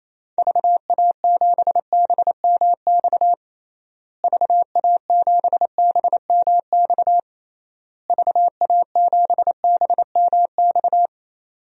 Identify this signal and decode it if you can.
Morse code, heard as a keyed tone, about 28 wpm, 700 Hz. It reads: VA7BMX VA7BMX VA7BMX
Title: DXCC-100 Call Signs - 6 Characters - Repeated 3x - 28wpm